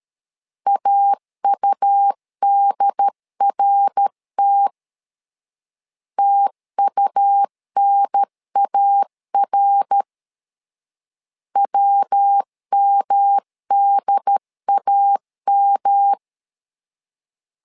Morse code, heard as a keyed tone, about 13 wpm, 800 Hz. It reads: AUDRT TUNAR WMDAM
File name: LEZIONI / 3 MWRO / 3 MWR 12wpm 1711